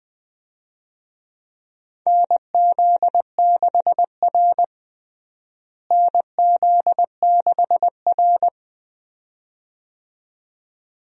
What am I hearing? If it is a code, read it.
Morse code, heard as a keyed tone, about 20 wpm, 700 Hz. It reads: NZ6R NZ6R
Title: USA Call Signs - 4 Characters - Repeated 2x - 20wpm